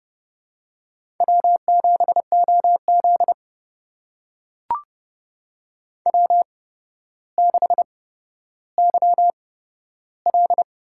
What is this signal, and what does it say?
Morse code, heard as a keyed tone, about 30 wpm, spaced as for 13 wpm, 700 Hz. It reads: W7OZ W6YL
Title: USA Call Signs - 4 Characters - 8x Character Spacing - 30wpm